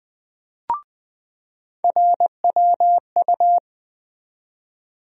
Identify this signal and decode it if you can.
Morse code, heard as a keyed tone, about 20 wpm, 700 Hz. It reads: RWU
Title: USA Call Sign Postfixes - 3 Letters 20wpm